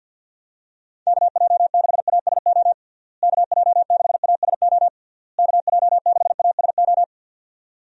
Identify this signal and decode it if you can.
Morse code, heard as a keyed tone, about 50 wpm, 700 Hz. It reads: KJ6ASO KJ6ASO KJ6ASO
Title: USA Call Signs - 6 Characters - Repeated 3x - 50wpm